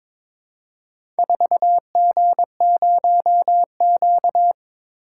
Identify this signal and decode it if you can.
Morse code, heard as a keyed tone, about 22 wpm, 700 Hz. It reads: 4G0Q
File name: practice / 22wpm / Four Letter-Number Combinations - 22wpm